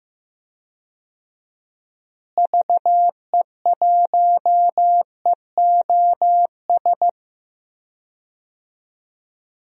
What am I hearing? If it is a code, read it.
Morse code, heard as a keyed tone, about 15 wpm, 700 Hz. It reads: VE1EOS
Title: DXCC-100 Call Signs - 6 Characters - Repeated 3x - 15wpm